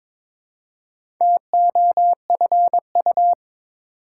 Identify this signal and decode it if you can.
Morse code, heard as a keyed tone, about 22 wpm, 700 Hz. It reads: TOFU
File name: Difficult Words with Hints - 22wpm